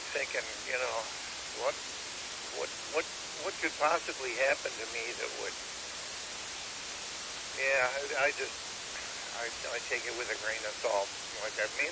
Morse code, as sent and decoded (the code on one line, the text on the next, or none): none